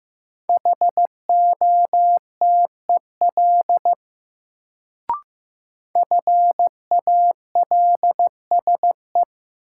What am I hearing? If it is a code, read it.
Morse code, heard as a keyed tone, about 15 wpm, 700 Hz. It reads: HOTEL FALSE
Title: Sentences from Top 1600 Words - New Word Drill - 15wpm